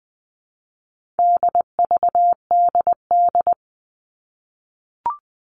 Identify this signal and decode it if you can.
Morse code, heard as a keyed tone, about 20 wpm, 700 Hz. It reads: DVDD
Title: All Cut Numbers - 4-Digit 20wpm